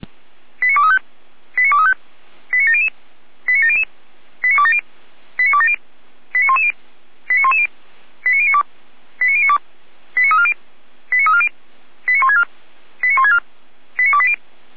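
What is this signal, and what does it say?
Five-tone selective call (ZVEI), heard as quick runs of five tones, tones of about 80 ms. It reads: 89235 89235 89700 89700 89270 89270 89110 89110 89092 89092 89340 89340 89164 89164 89280
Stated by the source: ZVEI five-tone call received on SDR.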